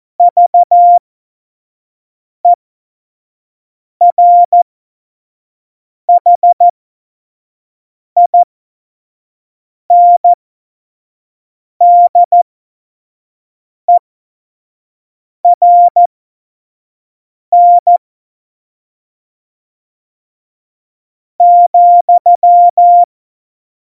Morse code, heard as a keyed tone, about 14 wpm, spaced as for 5 wpm, 700 Hz. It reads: VERHINDERN ,
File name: Morsekurs / Witze / Witz 09. Januar 2018, 25 F70